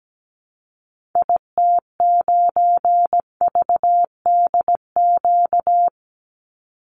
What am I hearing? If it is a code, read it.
Morse code, heard as a keyed tone, about 17 wpm, 700 Hz. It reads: IT9VDQ
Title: Call Signs - 17wpm